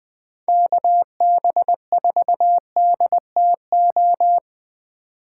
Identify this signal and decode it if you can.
Morse code, heard as a keyed tone, about 19 wpm, 700 Hz. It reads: KB4DTO
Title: Speed Racing - USA 2x3 Call Signs 17wpm